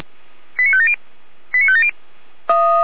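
Five-tone selective call (ZVEI), heard as quick runs of five tones, tones of about 70 ms. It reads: 89570 89570
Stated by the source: Fire brigade (Feuerwehr Orlamünde) ZVEI five-tone alert 88102, received on an SDR.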